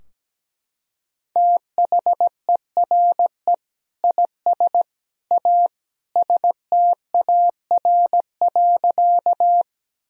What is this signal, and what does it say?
Morse code, heard as a keyed tone, about 17 wpm, 700 Hz. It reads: THERE IS A STAR.
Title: Sentences from Top 1000 Words - Review Part 2 - 17wpm